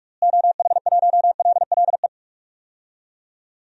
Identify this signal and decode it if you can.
Morse code, heard as a keyed tone, about 42 wpm, 700 Hz. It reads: OH1LLE
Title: DXCC-100 Call Signs - 6 Characters - Repeated 2x - 45wpm